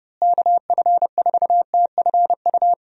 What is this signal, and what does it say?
Morse code, heard as a keyed tone, about 30 wpm, 700 Hz. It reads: KF4TFU
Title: USA Call Signs - 6 Characters - Repeated 3x - 30wpm